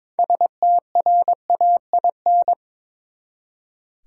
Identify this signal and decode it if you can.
Morse code, heard as a keyed tone, about 22 wpm, 700 Hz. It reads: STRAIN
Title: Six Letter Words - 22wpm